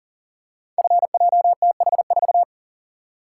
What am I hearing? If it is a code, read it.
Morse code, heard as a keyed tone, about 42 wpm, 690 Hz. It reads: FJTH4
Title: Code Groups - 5 Letter:Numbers - 40wpm